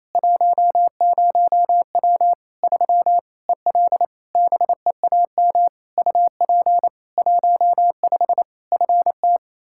Morse code, heard as a keyed tone, about 28 wpm, 700 Hz. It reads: 10W 3 EL BEAM UP 15 FT
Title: Practice QSO - Exchange 2 - Info - 28wpm